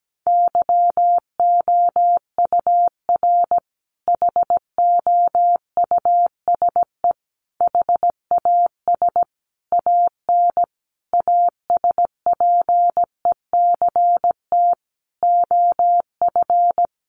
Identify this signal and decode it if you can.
Morse code, heard as a keyed tone, about 17 wpm, 700 Hz. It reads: YOUR HOUSE HAS AN ASPECT OF
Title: Sentences from Top 1300 Words 17wpm